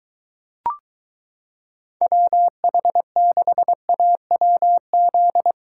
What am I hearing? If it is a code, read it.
Morse code, heard as a keyed tone, about 22 wpm, 700 Hz. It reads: WH6AWZ